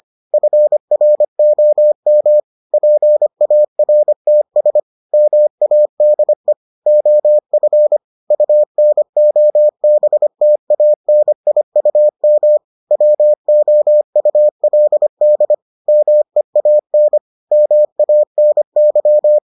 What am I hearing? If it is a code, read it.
Morse code, heard as a keyed tone, about 25 wpm, 600 Hz. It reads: FROM PARTS MADE OF UNOBTANIUM WOULD MEAN MANY